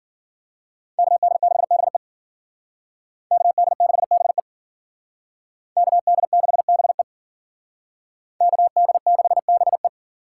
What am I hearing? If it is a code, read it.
Morse code, heard as a keyed tone, about 51 wpm, 700 Hz. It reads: KD6BE KD6BE KD6BE KD6BE